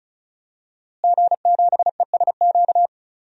Morse code, heard as a keyed tone, about 36 wpm, 700 Hz. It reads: G7ESQ
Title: DXCC-100 Call Signs - 5 Characters - Repeated 3x - 35wpm